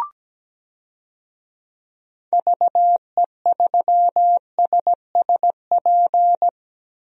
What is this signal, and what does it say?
Morse code, heard as a keyed tone, about 17 wpm, 700 Hz. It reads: VE3SSP
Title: DXCC-100 Call Signs - 6 Characters - Repeated 3x - 17wpm